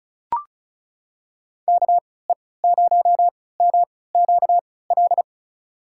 Morse code, heard as a keyed tone, about 36 wpm, spaced as for 20 wpm, 700 Hz. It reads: KE0MQL
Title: USA Call Signs - 6 Characters - 3x Character Spacing - 35wpm